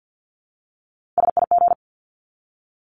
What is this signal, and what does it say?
Morse code, heard as a keyed tone, about 48 wpm, 700 Hz. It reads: SIG
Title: Abbreviations 50wpm